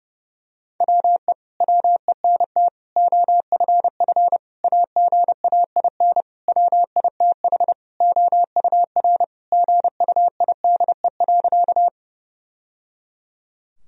Morse code, heard as a keyed tone, about 30 wpm, 700 Hz. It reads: WE WENT OFF AGAIN WITH OUR GUIDE.
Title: Sentences from Top 1000 Words - Review Part 3 - 30wpm